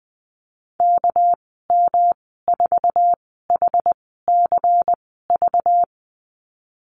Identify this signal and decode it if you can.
Morse code, heard as a keyed tone, about 19 wpm, spaced as for 14 wpm, 700 Hz. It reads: KM4HCV
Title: 213 Lesson 34g - Number 8 - Call Signs 20wpm